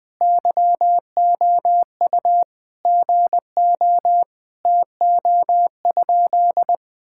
Morse code, heard as a keyed tone, about 20 wpm, 700 Hz. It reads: YOU GO TO?
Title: Sentences from Top 1000 Words - Review Part 5 - 20wpm